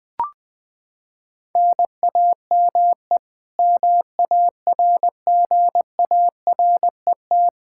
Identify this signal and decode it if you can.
Morse code, heard as a keyed tone, about 20 wpm, 700 Hz. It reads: NAME MARGARET
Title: Practice QSO - Exchange 1 - Beginning - 20wpm